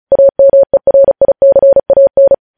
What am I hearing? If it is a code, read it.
Morse code, heard as a keyed tone, about 36 wpm, 550 Hz. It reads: AMERICAN
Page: News headlines in Morse code at 35 wpm